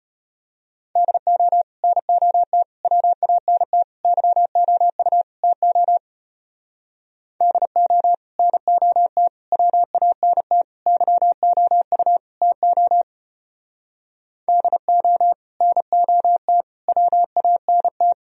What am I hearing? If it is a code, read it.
Morse code, heard as a keyed tone, about 33 wpm, 700 Hz. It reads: DO NOT WANT YOU TO DO NOT WANT YOU TO DO NOT WANT